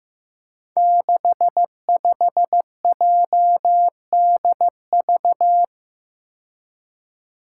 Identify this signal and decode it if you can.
Morse code, heard as a keyed tone, about 16 wpm, 700 Hz. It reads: 65JDV
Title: Five Challenging Letter-Number Combinations 15wpm